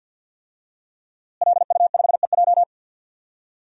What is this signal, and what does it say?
Morse code, heard as a keyed tone, about 49 wpm, 700 Hz. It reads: LU5EJ
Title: DXCC-100 Call Signs - 5 Characters - Repeated 2x - 50wpm